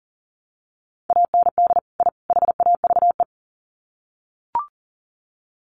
Morse code, heard as a keyed tone, about 40 wpm, 700 Hz. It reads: AND I HAVE
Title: Sets of 3 Words 40wpm